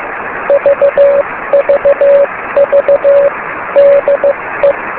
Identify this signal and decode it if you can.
Morse code, heard as a keyed tone, about 15 wpm, 560 Hz, with noise. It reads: VVV DE
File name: beacons / k9mu